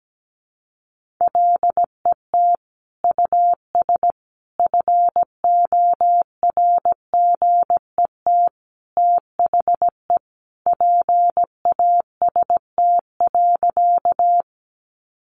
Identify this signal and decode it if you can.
Morse code, heard as a keyed tone, about 17 wpm, 700 Hz. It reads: LET US FORGET THE PAST.
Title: Sentences from Top 1200 Words 17wpm